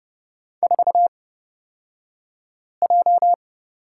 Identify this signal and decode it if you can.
Morse code, heard as a keyed tone, about 30 wpm, spaced as for 5 wpm, 700 Hz. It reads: 4J
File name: Lesson 38d - Letter q - Two Characters 30wpm